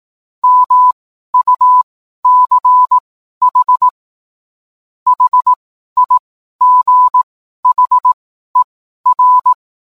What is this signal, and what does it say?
Morse code, heard as a keyed tone, about 18 wpm, spaced as for 12 wpm, 1000 Hz. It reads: MUCH HIGHER
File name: cwblog / cw mp3 directory / mem morse 12